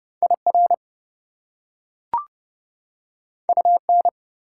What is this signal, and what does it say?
Morse code, heard as a keyed tone, about 30 wpm, 700 Hz. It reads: IR UN
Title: Prefixes and Suffixes 30wpm